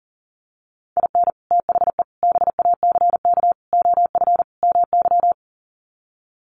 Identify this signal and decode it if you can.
Morse code, heard as a keyed tone, about 40 wpm, 700 Hz. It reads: IN THE BACK OF MY
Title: Sets of 5 Words Encore 40wpm